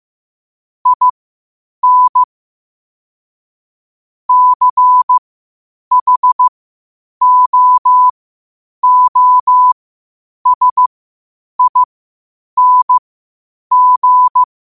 Morse code, heard as a keyed tone, about 15 wpm, spaced as for 8 wpm, 1000 Hz. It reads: IN CHOOSING
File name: qotdfiles / 8wpm